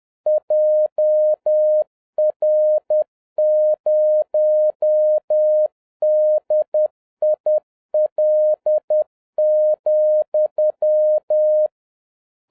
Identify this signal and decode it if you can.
Morse code, heard as a keyed tone, about 10 wpm, 610 Hz. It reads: JR0DIL,